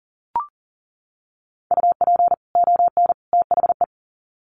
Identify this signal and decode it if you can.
Morse code, heard as a keyed tone, about 40 wpm, 700 Hz. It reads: UP ON THE
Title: Sets of 3 Words 40wpm